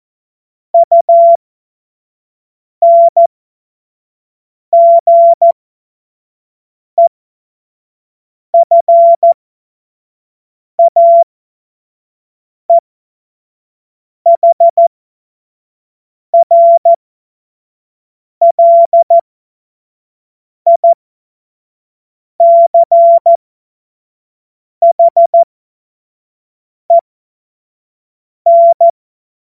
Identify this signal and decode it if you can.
Morse code, heard as a keyed tone, about 14 wpm, spaced as for 5 wpm, 680 Hz. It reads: UNGEFAEHRLICHEN
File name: Witz-09-September-2025-25-F70